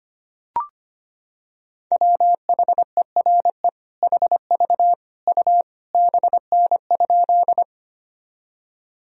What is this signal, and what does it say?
Morse code, heard as a keyed tone, about 25 wpm, 700 Hz. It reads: WHERE HV U BN?
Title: Sentences from Top 200 Words and Abbreviations 25wpm